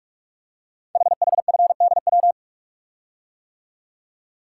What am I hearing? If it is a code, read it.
Morse code, heard as a keyed tone, about 44 wpm, 690 Hz. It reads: HHFDW
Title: Code Groups - 5 Letters - 45wpm